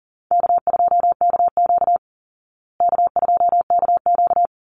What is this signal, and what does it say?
Morse code, heard as a keyed tone, about 40 wpm, 700 Hz. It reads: K2KQ K2KQ
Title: Call Signs Repeated 3x - 40wpm